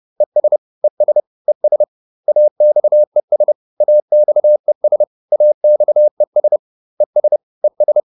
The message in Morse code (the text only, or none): ES ES ES AXES AXES AXES ES ES